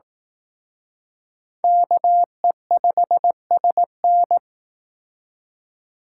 Morse code, heard as a keyed tone, about 19 wpm, 700 Hz. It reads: KE5SN